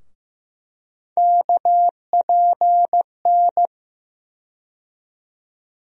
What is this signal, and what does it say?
Morse code, heard as a keyed tone, about 15 wpm, 700 Hz. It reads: KPN
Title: USA Call Sign Postfixes - 3 Letters 15wpm